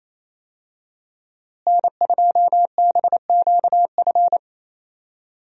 Morse code, heard as a keyed tone, about 27 wpm, 700 Hz. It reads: N2BQF
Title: USA Call Signs - 5 Characters - Repeated 3x - 28wpm